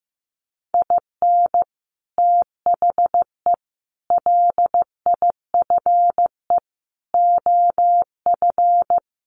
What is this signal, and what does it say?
Morse code, heard as a keyed tone, about 15 wpm, 700 Hz. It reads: IN THE LIFE OF
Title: Sets of 5 Words Encore2 15wpm